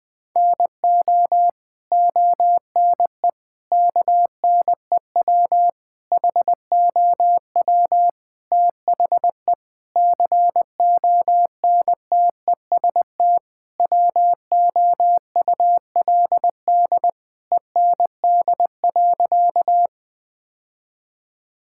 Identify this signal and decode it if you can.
Morse code, heard as a keyed tone, about 20 wpm, 700 Hz. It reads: NO ONE KNEW HOW THE CONTEST WOULD END.
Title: Sentences from Top 1600 Words 20wpm